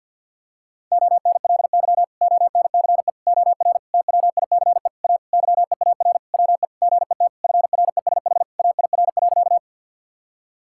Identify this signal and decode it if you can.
Morse code, heard as a keyed tone, about 50 wpm, 700 Hz. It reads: ONLY ONCE OR TWICE A YEAR WE GET FRESH AIR.
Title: Sentences from Top 1600 Words 50wpm